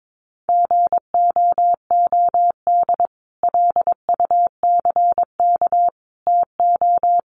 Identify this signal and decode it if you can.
Morse code, heard as a keyed tone, about 22 wpm, 700 Hz. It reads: GOOD LUCK TO
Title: Sentences from Top 1500 Words 22wpm